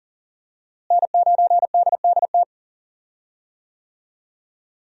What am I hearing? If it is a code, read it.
Morse code, heard as a keyed tone, about 40 wpm, 700 Hz. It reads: N9DDT
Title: USA Call Signs - 5 Characters - Repeated 3x - 40wpm